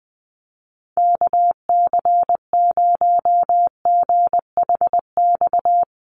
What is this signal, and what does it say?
Morse code, heard as a keyed tone, about 20 wpm, 700 Hz. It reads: KC0GHX
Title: USA Call Signs Repeated 3x 20wpm